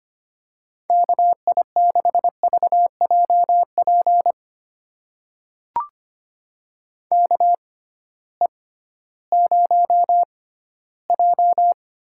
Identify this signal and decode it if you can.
Morse code, heard as a keyed tone, about 25 wpm, spaced as for 15 wpm, 700 Hz. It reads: KI6VJP KE0J
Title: USA Call Signs - 6 Characters - 6x Character Spacing - 25wpm